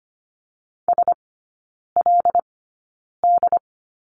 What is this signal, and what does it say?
Morse code, heard as a keyed tone, about 25 wpm, spaced as for 9 wpm, 700 Hz. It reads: SLD
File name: Lesson 15e - Number 5 - Three Characters 25wpm